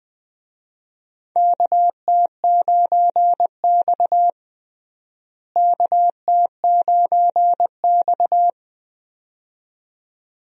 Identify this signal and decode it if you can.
Morse code, heard as a keyed tone, about 20 wpm, 700 Hz. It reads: KT9X KT9X
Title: USA Call Signs - 4 Characters - Repeated 2x - 20wpm